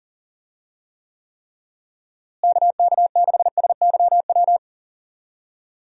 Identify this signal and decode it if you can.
Morse code, heard as a keyed tone, about 38 wpm, 700 Hz. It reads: KK6SYW
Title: USA Call Signs - 6 Characters - Repeated 3x - 40wpm